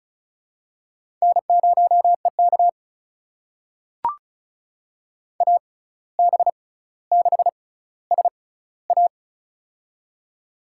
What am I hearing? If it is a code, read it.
Morse code, heard as a keyed tone, about 36 wpm, spaced as for 17 wpm, 700 Hz. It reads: N0EK AB6SA
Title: USA Call Signs - 6x Character Spacing - 35wpm